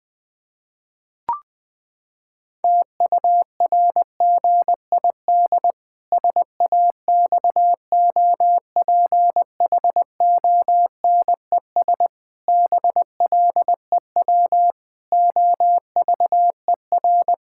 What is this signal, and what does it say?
Morse code, heard as a keyed tone, about 21 wpm, 700 Hz. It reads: TURGID SAXOPHONES BLEW OVER
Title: Pangram Challenge v2 - 20wpm